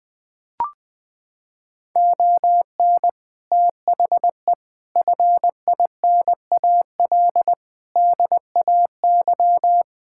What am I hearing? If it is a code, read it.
Morse code, heard as a keyed tone, about 20 wpm, 700 Hz. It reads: ON THE FINAL DAY
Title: Sets of 6 Words - 20wpm